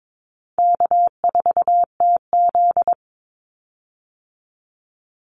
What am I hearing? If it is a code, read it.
Morse code, heard as a keyed tone, about 22 wpm, 700 Hz. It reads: K4TZ
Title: Easy Call Signs Repeated 2x - 22wpm